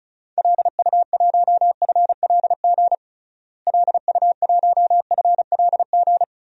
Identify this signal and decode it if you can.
Morse code, heard as a keyed tone, about 36 wpm, 700 Hz. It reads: LU1FLG LU1FLG
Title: DXCC-100 Call Signs - 6 Characters - Repeated 3x - 35wpm